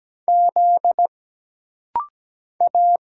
Morse code, heard as a keyed tone, about 17 wpm, 700 Hz. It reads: Z A
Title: Single Letters - Rapid-Fire 17wpm